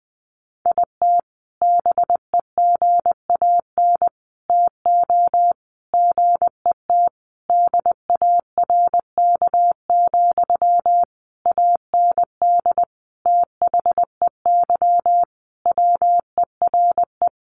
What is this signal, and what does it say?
Morse code, heard as a keyed tone, about 20 wpm, 700 Hz. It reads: IT BEGAN TO GET DARK, AND THEY WERE
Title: Sentences from Top 1400 Words 20wpm